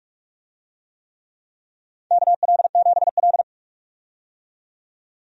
Morse code, heard as a keyed tone, about 44 wpm, 700 Hz. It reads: KL7L
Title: DXCC-100 Call Signs - 4 Characters - 45wpm